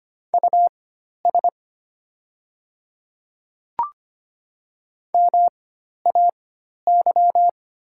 Morse code, heard as a keyed tone, about 25 wpm, spaced as for 12 wpm, 690 Hz. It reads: US MAY
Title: 4x Character Spacing - Top 200 Words - 25wpm